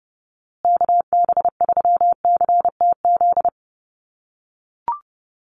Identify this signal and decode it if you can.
Morse code, heard as a keyed tone, about 30 wpm, 700 Hz. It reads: KB3CTZ
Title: USA Call Signs Repeated 3x 30wpm